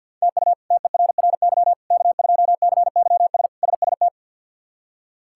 Morse code, heard as a keyed tone, about 52 wpm, 700 Hz. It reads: TU TERRY K2CYS SST
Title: K1USN SST QSO Practice - 50wpm